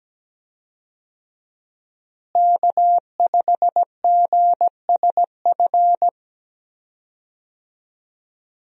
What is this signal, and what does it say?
Morse code, heard as a keyed tone, about 17 wpm, 700 Hz. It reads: K5GSF
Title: USA Call Signs - 5 Characters - Repeated 3x - 17wpm